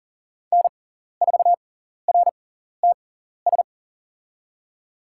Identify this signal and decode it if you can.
Morse code, heard as a keyed tone, about 40 wpm, spaced as for 14 wpm, 700 Hz. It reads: N4RTS
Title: USA Call Signs - 5 Characters - 6x Character Spacing - 40wpm